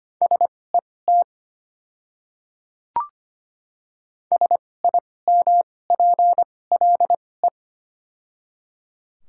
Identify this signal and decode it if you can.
Morse code, heard as a keyed tone, about 25 wpm, spaced as for 18 wpm, 700 Hz. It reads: SET SIMPLE
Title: 2x Character Spacing - Top 500 Words - 25wpm